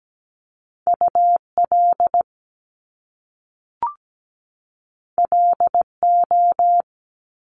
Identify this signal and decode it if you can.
Morse code, heard as a keyed tone, about 17 wpm, 700 Hz. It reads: UL LO
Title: Top Two Letter Combinations in Words 17wpm